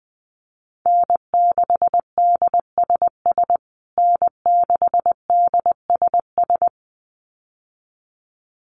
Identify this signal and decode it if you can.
Morse code, heard as a keyed tone, about 19 wpm, 700 Hz. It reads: N6DSS N6DSS
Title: USA Call Signs Repeated 2x 20wpm